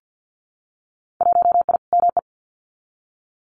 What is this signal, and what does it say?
Morse code, heard as a keyed tone, about 50 wpm, 700 Hz. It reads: 1I ME
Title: Winter Field Day Exchanges Encore 50wpm